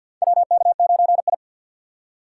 Morse code, heard as a keyed tone, about 52 wpm, 700 Hz. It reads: WK9I
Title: USA Call Signs - 4x Character Spacing - 50wpm